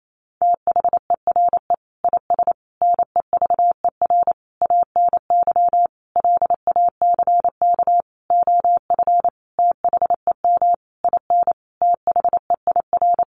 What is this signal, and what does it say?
Morse code, heard as a keyed tone, about 28 wpm, 700 Hz. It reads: THERE IS NEVER ANY LACK OF THEM IN THEIR
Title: Sentences from Top 1100 Words 28wpm